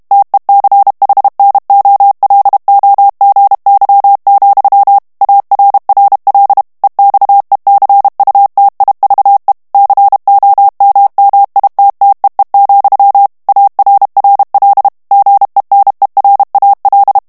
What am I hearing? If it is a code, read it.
Morse code, heard as a keyed tone, about 32 wpm, 800 Hz. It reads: TECHNOLOGY, ARRL EXECUTIVE COMMITTEE, ARRL GENERAL
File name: ANGLAIS texte clair 3 vitesse 30 mots minute